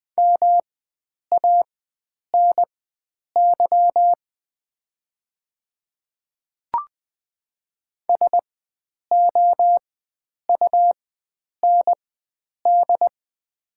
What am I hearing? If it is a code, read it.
Morse code, heard as a keyed tone, about 20 wpm, spaced as for 9 wpm, 700 Hz. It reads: MANY SOUND
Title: 4x Character Spacing - Top 100 Words - 20wpm